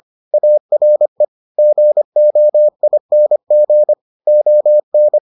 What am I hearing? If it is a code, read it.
Morse code, heard as a keyed tone, about 25 wpm, 600 Hz. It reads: ARE GOING ON